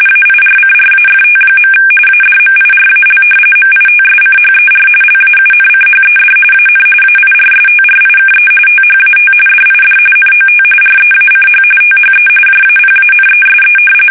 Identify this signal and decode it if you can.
RTTY, about 45.45 baud. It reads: AIRCRAFT. ADMIRAL MITSCHER LAUNCHED AN AIRSTRIKE THAT DID NOT REACH THE JAPANESE
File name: fsk850 / battleshipnj